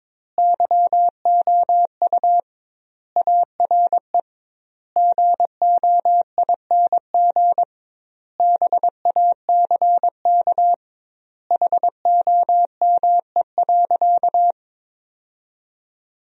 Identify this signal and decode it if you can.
Morse code, heard as a keyed tone, about 22 wpm, 700 Hz. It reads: YOU ARE GOING BACK HOME.
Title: 2x Word-Spacing - Sentences from Top 500 Words - 22wpm